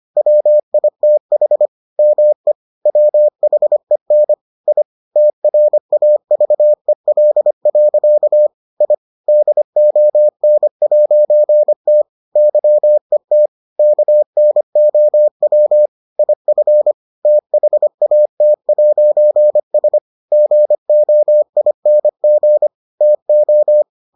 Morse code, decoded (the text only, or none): WITH ME WHEN I TRAVEL. I DON'T YET KNOW IF THAT'S GOING TO